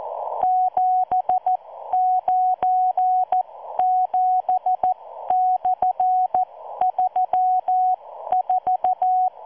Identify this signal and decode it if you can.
Morse code, heard as a keyed tone, about 14 wpm, spaced as for 12 wpm, 740 Hz. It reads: 797/34